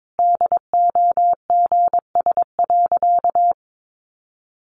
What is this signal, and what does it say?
Morse code, heard as a keyed tone, about 22 wpm, 700 Hz. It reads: DOGS.
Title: Additional Easy Phrases 22wpm